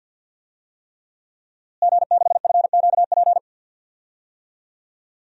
Morse code, heard as a keyed tone, about 52 wpm, 700 Hz. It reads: G6FQP
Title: DXCC-100 Call Signs - 5 Characters - Repeated 3x - 50wpm